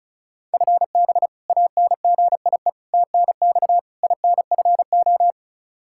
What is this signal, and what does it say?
Morse code, heard as a keyed tone, about 34 wpm, 700 Hz. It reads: FB ANGIE TNX INFO
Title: K1USN SST QSO Practice 35wpm